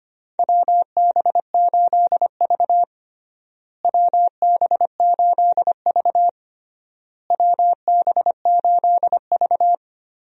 Morse code, heard as a keyed tone, about 25 wpm, 700 Hz. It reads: WB8V WB8V WB8V
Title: USA Call Signs - 4 Characters - Repeated 3x - 25wpm